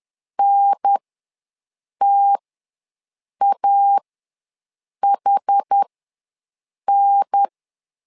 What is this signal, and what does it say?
Morse code, heard as a keyed tone, about 10 wpm, spaced as for 6 wpm, 800 Hz. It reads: NTAHN
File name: 2 UHDN ws 10wpm